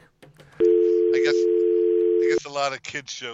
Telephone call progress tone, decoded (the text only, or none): dial tone